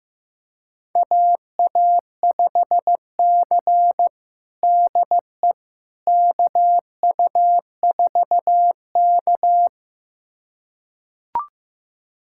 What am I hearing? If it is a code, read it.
Morse code, heard as a keyed tone, about 15 wpm, 700 Hz. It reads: AA5C DE KU4K
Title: USA Callsign DE Callsign - 4 Characters - 15wpm